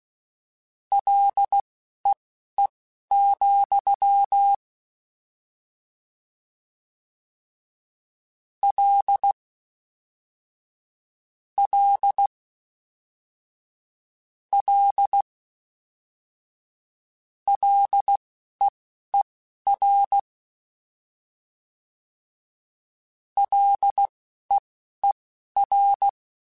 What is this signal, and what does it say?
Morse code, heard as a keyed tone, about 16 wpm, spaced as for 11 wpm, 800 Hz. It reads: LEE, L L L LEER LEER